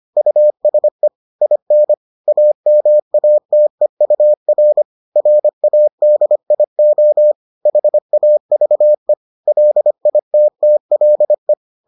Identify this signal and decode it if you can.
Morse code, heard as a keyed tone, about 25 wpm, 600 Hz. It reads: USE IN AMATEUR RADIO HAVE LITTLE